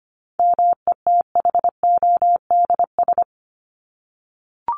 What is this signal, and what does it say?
Morse code, heard as a keyed tone, about 25 wpm, 700 Hz. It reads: METHODS
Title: ES:S Suffix - Top 100 Words with Base Words 25wpm